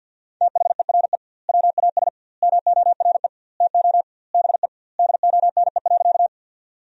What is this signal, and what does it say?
Morse code, heard as a keyed tone, about 46 wpm, 700 Hz. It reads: THERE WAS MORE TO BE DONE.
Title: 2x Word-Spacing - Sentences from Top 500 Words - 50wpm